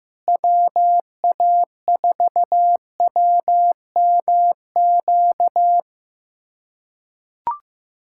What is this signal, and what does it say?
Morse code, heard as a keyed tone, about 15 wpm, 700 Hz. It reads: WA4WMQ
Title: USA Callsign DE Callsign - 6 Characters - 15wpm